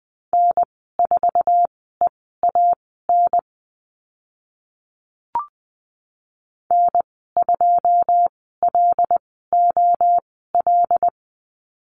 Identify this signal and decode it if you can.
Morse code, heard as a keyed tone, about 20 wpm, spaced as for 14 wpm, 700 Hz. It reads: N4EAN N2LOL
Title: 082 Lesson 14g - Number 2 - Call Signs 20wpm